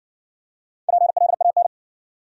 Morse code, heard as a keyed tone, about 58 wpm, 700 Hz. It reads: PLAN